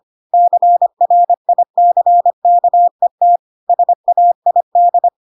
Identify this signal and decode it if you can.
Morse code, heard as a keyed tone, about 25 wpm, 700 Hz. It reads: CRICKET SAID
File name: Pinocchio / Pinocchio ch7 25 25 ik1hge